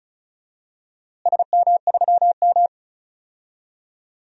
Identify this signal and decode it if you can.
Morse code, heard as a keyed tone, about 34 wpm, 700 Hz. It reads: SM3M
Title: DXCC-100 Call Signs - 4 Characters - Repeated 2x - 35wpm